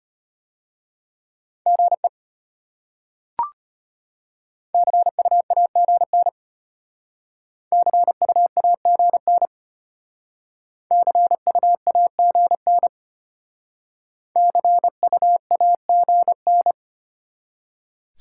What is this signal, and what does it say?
Morse code, heard as a keyed tone, about 32 wpm, 700 Hz. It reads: GE CUAGN CUAGN CUAGN CUAGN